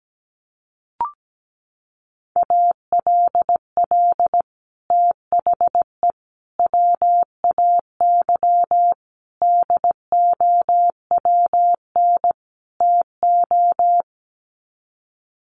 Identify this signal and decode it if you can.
Morse code, heard as a keyed tone, about 17 wpm, 700 Hz. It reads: ALL THE WAY DOWN TO
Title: Sets of 5 Words Encore2 17wpm